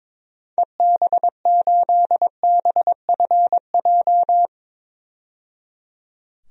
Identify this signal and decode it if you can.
Morse code, heard as a keyed tone, about 22 wpm, 700 Hz. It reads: EB8BFJ
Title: DXCC-100 Call Signs - 6 Characters - 22wpm